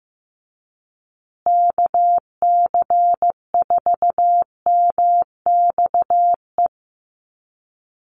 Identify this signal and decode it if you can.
Morse code, heard as a keyed tone, about 15 wpm, 700 Hz. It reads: KC4MXE
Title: USA Call Signs Repeated 3x 15wpm